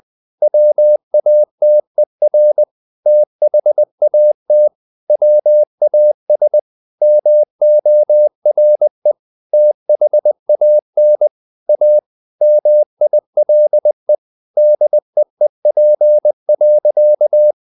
Morse code, heard as a keyed tone, about 20 wpm, 600 Hz. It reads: WATER THAT WAS MORE THAN A MILE DEEP.
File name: the old man and the sea part 2 20wpm